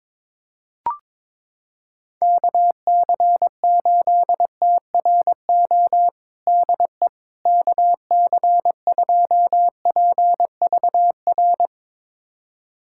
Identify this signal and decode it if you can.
Morse code, heard as a keyed tone, about 22 wpm, 700 Hz. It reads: KC8TRO DE KC2PVR
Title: USA Callsign DE Callsign - 6 Characters - 22wpm